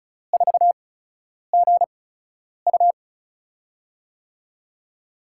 Morse code, heard as a keyed tone, about 34 wpm, spaced as for 10 wpm, 700 Hz. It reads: 4GU
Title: USA Call Signs - 8x Character Spacing - 35wpm